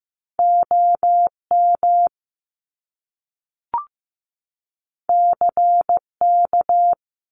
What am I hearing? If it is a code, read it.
Morse code, heard as a keyed tone, about 15 wpm, 700 Hz. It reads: OM CK
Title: Top Two Letter Combinations in Words 15wpm